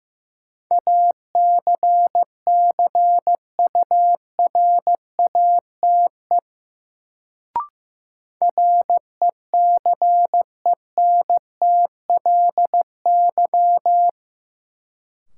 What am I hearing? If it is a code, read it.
Morse code, heard as a keyed tone, about 15 wpm, 700 Hz. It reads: ACCURATE RECENTLY
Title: Sentences from Top 1200 Words - New Word Drill - 15wpm